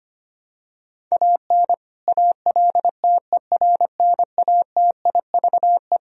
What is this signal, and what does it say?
Morse code, heard as a keyed tone, about 25 wpm, 700 Hz. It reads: AN ALTERNATIVE
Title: Sentences from Top 1000 Words - Review Part 4 - 25wpm